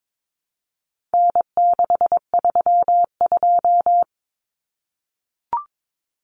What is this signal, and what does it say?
Morse code, heard as a keyed tone, about 22 wpm, 700 Hz. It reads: N632
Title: Common Cut Numbers - 4-Digit 22wpm